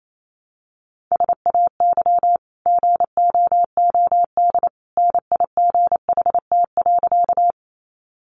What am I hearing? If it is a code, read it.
Morse code, heard as a keyed tone, about 28 wpm, 700 Hz. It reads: SAY GOOD NIGHT.
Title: Sentences from Top 300 Words 28wpm